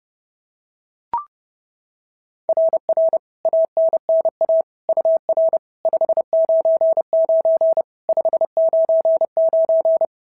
Morse code, heard as a keyed tone, about 30 wpm, 690 Hz. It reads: RR ANNA UR 599 599
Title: SOTA QSO Practice 30wpm